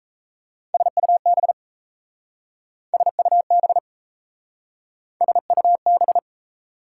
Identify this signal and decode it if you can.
Morse code, heard as a keyed tone, about 37 wpm, 700 Hz. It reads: SUB SUB SUB